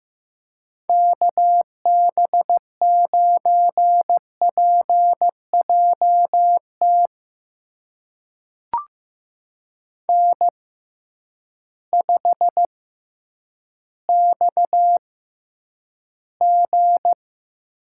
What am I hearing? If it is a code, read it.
Morse code, heard as a keyed tone, about 15 wpm, spaced as for 9 wpm, 700 Hz. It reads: KB9PJT N5XG
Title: USA Call Signs - 6x Character Spacing - 15wpm